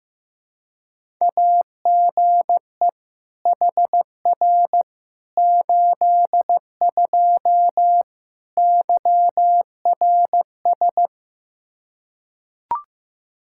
Morse code, heard as a keyed tone, about 15 wpm, 700 Hz. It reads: AGE HR 82 YRS
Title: K1USN SST QSO Practice 15wpm